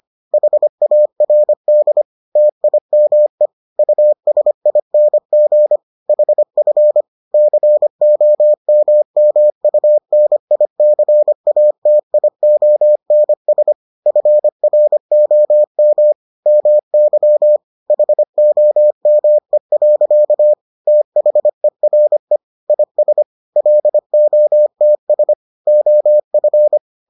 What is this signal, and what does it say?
Morse code, heard as a keyed tone, about 25 wpm, 600 Hz. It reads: HARD TIME USING HF COMMUNICATIONS FROM MY HOME. THERE IS LOTS OF